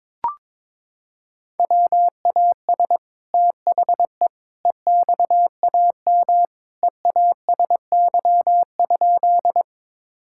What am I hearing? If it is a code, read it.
Morse code, heard as a keyed tone, about 22 wpm, 700 Hz. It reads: WAS THE EXAM EASY?
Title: Sentences from Top 1000 Words - Review Part 5 - 22wpm